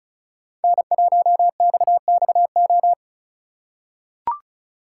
Morse code, heard as a keyed tone, about 34 wpm, 700 Hz. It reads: N1XXO
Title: USA Call Signs - 5 Characters - 4x Character Spacing - 35wpm